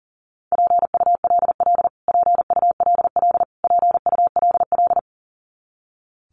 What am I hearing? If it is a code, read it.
Morse code, heard as a keyed tone, about 40 wpm, 700 Hz. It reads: PULL PULL PULL
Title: Top 500 Words Repeated 40wpm